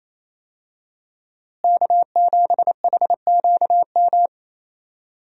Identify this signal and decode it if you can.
Morse code, heard as a keyed tone, about 28 wpm, 700 Hz. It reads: K7HQM
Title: USA Call Signs - 5 Characters - Repeated 2x - 28wpm